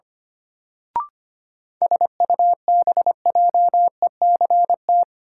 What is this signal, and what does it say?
Morse code, heard as a keyed tone, about 25 wpm, 700 Hz. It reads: SUBJECT
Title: Sentences from Top 800 Words - New Word Drill - 25wpm